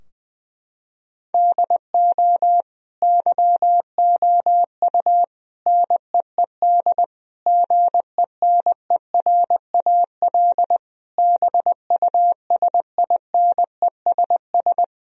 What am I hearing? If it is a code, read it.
Morse code, heard as a keyed tone, about 20 wpm, 700 Hz. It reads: DO YOU NEED GENERAL BUSINESS
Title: Sentences from Top 1000 Words - Review Part 3 - 20wpm